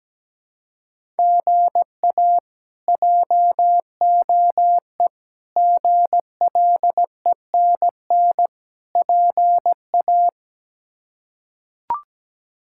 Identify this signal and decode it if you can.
Morse code, heard as a keyed tone, about 17 wpm, 700 Hz. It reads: GA JOE GLENN PA
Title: K1USN SST QSO Practice - 17wpm